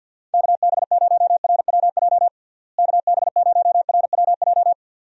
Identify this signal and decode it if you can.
Morse code, heard as a keyed tone, about 51 wpm, 700 Hz. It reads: KB0RWJ KB0RWJ
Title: USA Call Signs - 6 Characters - Repeated 2x - 50wpm